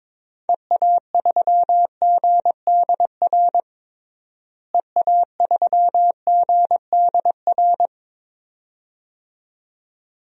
Morse code, heard as a keyed tone, about 22 wpm, 700 Hz. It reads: EA3GDR EA3GDR
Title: DXCC-100 Call Signs - 6 Characters - Repeated 2x - 22wpm